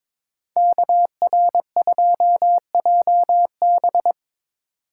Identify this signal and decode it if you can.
Morse code, heard as a keyed tone, about 25 wpm, 700 Hz. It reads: KR2JB
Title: Speed Racing - USA 2x2 Call Signs 22wpm